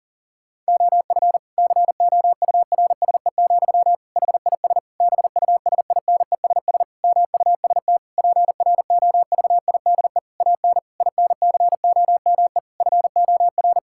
Morse code, heard as a keyed tone, about 37 wpm, 700 Hz. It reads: OF COURSE, HIS BUSINESS MUST PROVIDE AN INCOME FOR
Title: Sentences from Top 1000 Words - Review Part 5 - 40wpm